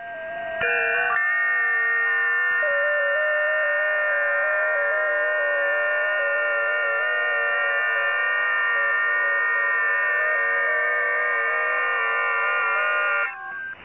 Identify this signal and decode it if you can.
FT8.
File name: tests / websdr test7